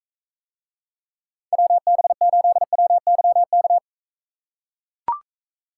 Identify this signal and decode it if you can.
Morse code, heard as a keyed tone, about 43 wpm, 700 Hz. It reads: WB8WYK